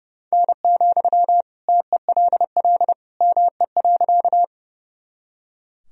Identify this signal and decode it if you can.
Morse code, heard as a keyed tone, about 30 wpm, 700 Hz. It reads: N, TELL ME.
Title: Sentences from Top 200 Words and Abbreviations 30wpm